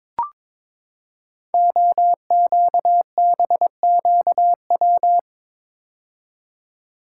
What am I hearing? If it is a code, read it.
Morse code, heard as a keyed tone, about 23 wpm, 700 Hz. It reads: OQBQW
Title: Code Groups - 5 Letters - 22wpm